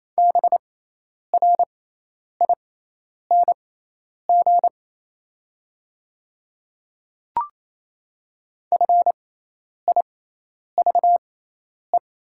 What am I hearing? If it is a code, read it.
Morse code, heard as a keyed tone, about 27 wpm, spaced as for 10 wpm, 690 Hz. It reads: BRING FIVE
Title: 6x Character Spacing - Top 500 Words - 28wpm